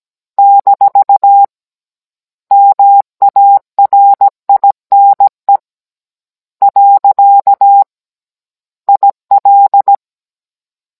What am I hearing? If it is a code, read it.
Morse code, heard as a keyed tone, about 17 wpm, 800 Hz. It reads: - MARINE . IL
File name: Cours-NR19-16mm